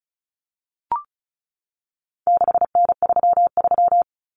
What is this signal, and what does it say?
Morse code, heard as a keyed tone, about 34 wpm, 700 Hz. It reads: 6N33
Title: Common Cut Numbers - 4-Digit 35wpm